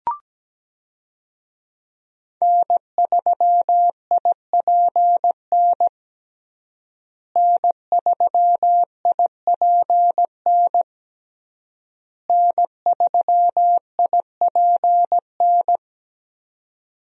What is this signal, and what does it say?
Morse code, heard as a keyed tone, about 17 wpm, 700 Hz. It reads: N3IPN N3IPN N3IPN
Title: USA Call Signs - 5 Characters - Repeated 3x - 17wpm